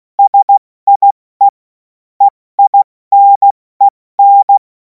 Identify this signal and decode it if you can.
Morse code, heard as a keyed tone, about 16 wpm, spaced as for 14 wpm, 800 Hz. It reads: SIE EINEN
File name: Morsekurs / Witze / Witz 07. Juli 2026 70 F80